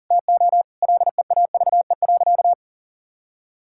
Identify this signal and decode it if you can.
Morse code, heard as a keyed tone, about 37 wpm, 700 Hz. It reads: TO LEAVE.
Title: Sentences from Top 1000 Words - Review Part 5 - 40wpm